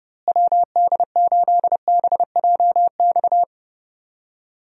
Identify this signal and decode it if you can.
Morse code, heard as a keyed tone, about 30 wpm, 700 Hz. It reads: WD8BJX
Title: USA Call Signs - 6 Characters - Repeated 3x - 30wpm